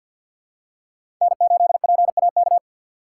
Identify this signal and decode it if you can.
Morse code, heard as a keyed tone, about 52 wpm, 700 Hz. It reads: N8PAK
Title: USA Call Signs - 5 Characters - 6x Character Spacing - 50wpm